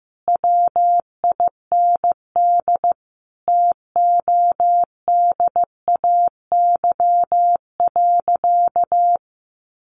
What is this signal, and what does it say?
Morse code, heard as a keyed tone, about 14 wpm, 700 Hz. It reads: WIND TODAY.
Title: Sentences from Top 1400 Words 15wpm